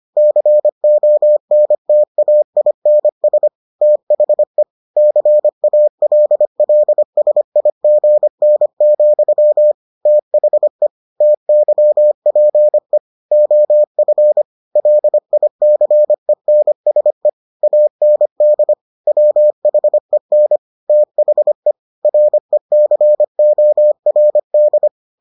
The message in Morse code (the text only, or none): CONTAINS THE CALLSIGN, THE TYPE OF LICENSE AND WHEN THE RECORD